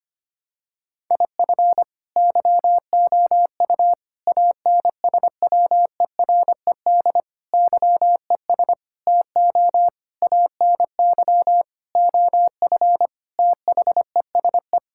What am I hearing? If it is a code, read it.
Morse code, heard as a keyed tone, about 25 wpm, 700 Hz. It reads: IF YOU ANSWERED YES TO ANY OF THESE
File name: Sets of 9 Words - 25wpm